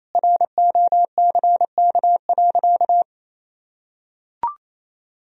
Morse code, heard as a keyed tone, about 28 wpm, 700 Hz. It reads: ROCK.
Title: Sentences from Top 1000 Words - Review Part 2 - 28wpm